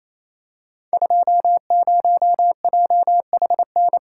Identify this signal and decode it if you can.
Morse code, heard as a keyed tone, about 28 wpm, 700 Hz. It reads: 20JHN
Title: Code Groups - 5 Letter:Numbers - 28wpm